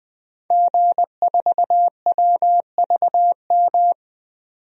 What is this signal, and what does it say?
Morse code, heard as a keyed tone, about 20 wpm, 700 Hz. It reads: G4WVM
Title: DXCC-100 Call Signs - 5 Characters - 20wpm